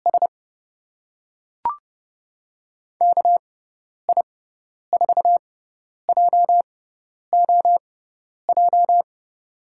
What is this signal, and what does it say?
Morse code, heard as a keyed tone, about 31 wpm, spaced as for 10 wpm, 700 Hz. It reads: S KI4JOJ
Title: USA Call Signs - 6x Character Spacing - 30wpm